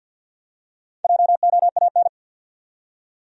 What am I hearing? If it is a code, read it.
Morse code, heard as a keyed tone, about 52 wpm, 700 Hz. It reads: JOAN
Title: USA Common Names - Max 5 Letters - 50wpm